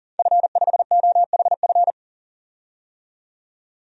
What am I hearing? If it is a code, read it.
Morse code, heard as a keyed tone, about 40 wpm, 700 Hz. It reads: F5OHF
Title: DXCC-100 Call Signs - 5 Characters - 40wpm